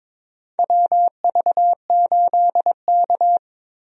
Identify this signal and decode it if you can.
Morse code, heard as a keyed tone, about 22 wpm, 700 Hz. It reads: WV8K
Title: USA Call Signs - 4 Characters - Repeated 3x - 22wpm